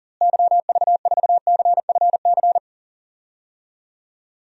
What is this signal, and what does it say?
Morse code, heard as a keyed tone, about 40 wpm, 700 Hz. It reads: YV4CFC
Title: DXCC-100 Call Signs - 6 Characters - Repeated 2x - 40wpm